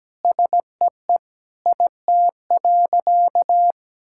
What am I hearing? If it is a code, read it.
Morse code, heard as a keyed tone, about 17 wpm, 700 Hz. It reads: SEE IT.
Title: Sentences from Top 1600 Words 17wpm